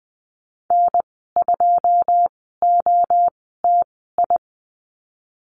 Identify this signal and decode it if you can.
Morse code, heard as a keyed tone, about 20 wpm, spaced as for 15 wpm, 700 Hz. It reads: N2OTI
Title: 088 Lesson 15g - Number 5 - Call Signs 20wpm